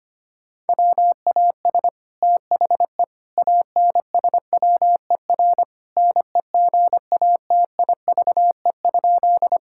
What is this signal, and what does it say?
Morse code, heard as a keyed tone, about 25 wpm, 700 Hz. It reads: WAS THE ANSWER NEGATIVE?
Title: Sentences from Top 1000 Words - Review Part 6 - 25wpm